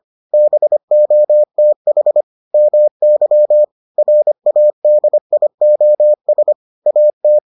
Morse code, heard as a keyed tone, about 25 wpm, 600 Hz. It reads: BOTH MY RADIOS AT